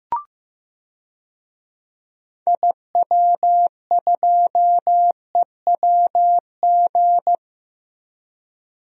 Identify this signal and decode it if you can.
Morse code, heard as a keyed tone, about 15 wpm, 700 Hz. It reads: IW2EWG
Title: DXCC-100 Call Signs - 6 Characters - Repeated 3x - 15wpm